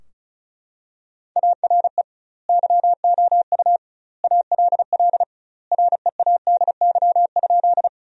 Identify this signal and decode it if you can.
Morse code, heard as a keyed tone, about 36 wpm, 700 Hz. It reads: ARE YOU ALL READY?
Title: 2x Word-Spacing - Sentences from Top 400 Words - 35wpm